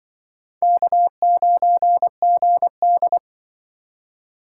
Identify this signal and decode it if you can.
Morse code, heard as a keyed tone, about 23 wpm, 700 Hz. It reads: K9GD
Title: Speed Racing - USA 1x2 Call Signs 20wpm